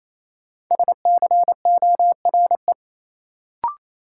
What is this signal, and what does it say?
Morse code, heard as a keyed tone, about 28 wpm, 700 Hz. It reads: SCORE
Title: Sentences from Top 1600 Words - New Word Drill Alt - 28wpm